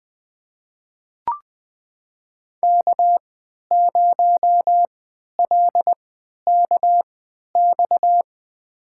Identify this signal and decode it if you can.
Morse code, heard as a keyed tone, about 20 wpm, spaced as for 11 wpm, 700 Hz. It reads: K0LKX
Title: USA Call Signs - 5 Characters - 3x Character Spacing - 20wpm